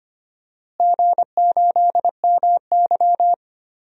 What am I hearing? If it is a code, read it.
Morse code, heard as a keyed tone, about 25 wpm, 700 Hz. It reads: G8MY
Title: DXCC-100 Call Signs - 4 Characters - Repeated 3x - 25wpm